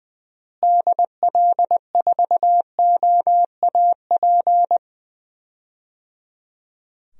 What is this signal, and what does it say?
Morse code, heard as a keyed tone, about 20 wpm, 700 Hz. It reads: DL4OAP
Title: USA Call Signs - 6 Characters - Repeated 3x - 20wpm